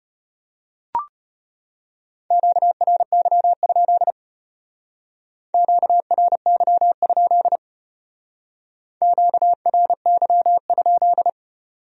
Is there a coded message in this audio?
Yes: Morse code from 2.3 to 11.3 s, heard as a keyed tone, about 32 wpm, 700 Hz. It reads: QRY? QRY? QRY?